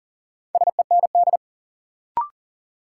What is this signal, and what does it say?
Morse code, heard as a keyed tone, about 40 wpm, 700 Hz. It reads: SEND
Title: Sentences from Top 1500 Words - New Word Drill Alt - 40wpm